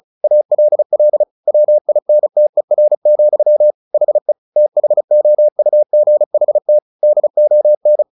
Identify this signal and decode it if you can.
Morse code, heard as a keyed tone, about 34 wpm, 600 Hz. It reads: ALL WINTER, HE THOUGHT DON
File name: the old man and the sea 2 35wpm